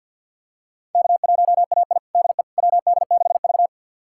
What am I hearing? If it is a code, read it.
Morse code, heard as a keyed tone, about 46 wpm, 700 Hz. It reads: K1AI DE WD6V
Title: USA Callsign DE Callsign - 4 Characters - 50wpm